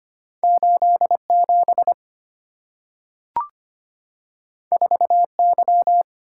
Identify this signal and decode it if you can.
Morse code, heard as a keyed tone, about 25 wpm, 700 Hz. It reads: 8 7 4 Y
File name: Two Letter-Number Combinations - 25wpm